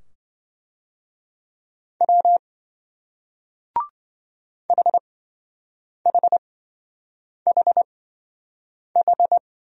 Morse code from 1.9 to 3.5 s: W

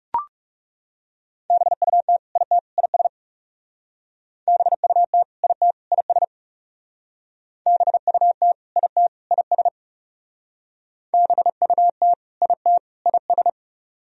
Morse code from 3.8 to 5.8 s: BUT IT

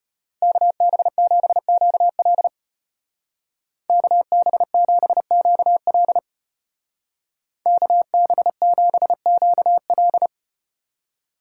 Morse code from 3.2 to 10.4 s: KB7QL KB7QL

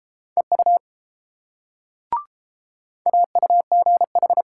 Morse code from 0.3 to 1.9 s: EU

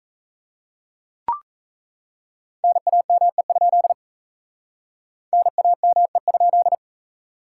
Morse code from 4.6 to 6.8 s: NAME?

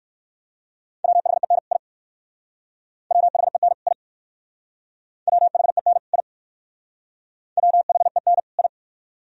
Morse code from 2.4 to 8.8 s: WHEN I WHEN I WHEN I

